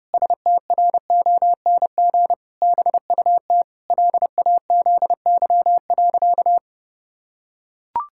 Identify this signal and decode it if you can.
Morse code, heard as a keyed tone, about 30 wpm, 700 Hz. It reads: STRONG BUT LAZY.
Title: Pangram Challenge v2 - 30wpm